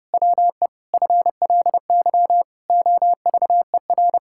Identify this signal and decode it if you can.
Morse code, heard as a keyed tone, about 30 wpm, 700 Hz. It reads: WE FLY OVER
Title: Pangram Challenge v2 - 30wpm